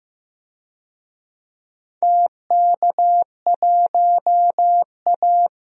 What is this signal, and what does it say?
Morse code, heard as a keyed tone, about 15 wpm, 700 Hz. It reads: TK1A
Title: DXCC-100 Call Signs - 4 Characters - 15wpm